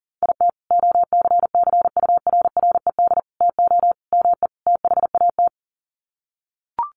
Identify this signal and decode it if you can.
Morse code, heard as a keyed tone, about 40 wpm, 700 Hz. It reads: IT OCCURRED TO ME THAT
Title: Sets of 5 Words Encore 40wpm